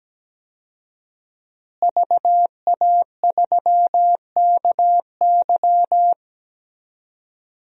Morse code, heard as a keyed tone, about 17 wpm, 700 Hz. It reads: VA3KY
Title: DXCC-100 Call Signs - 5 Characters - Repeated 3x - 17wpm